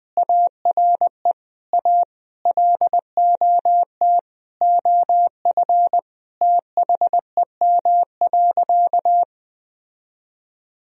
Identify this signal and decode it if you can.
Morse code, heard as a keyed tone, about 20 wpm, 700 Hz. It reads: ARE A LOT OF THEM.
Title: Sentences from Top 1000 Words - Review Part 3 - 20wpm